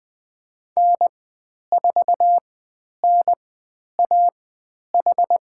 Morse code, heard as a keyed tone, about 20 wpm, spaced as for 10 wpm, 700 Hz. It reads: N4NAH
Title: 102 Lesson 17g - Letter u - Call Signs 20wpm